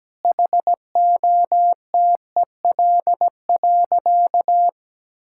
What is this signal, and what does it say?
Morse code, heard as a keyed tone, about 17 wpm, 700 Hz. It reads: HOTEL.
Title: Sentences from Top 1600 Words 17wpm